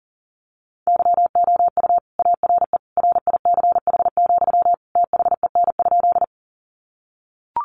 Morse code, heard as a keyed tone, about 40 wpm, 700 Hz. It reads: YOU ARE RICH, THEN?
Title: Sentences from Top 1100 Words 40wpm